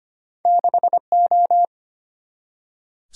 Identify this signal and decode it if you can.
Morse code, heard as a keyed tone, about 34 wpm, 700 Hz. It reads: 6O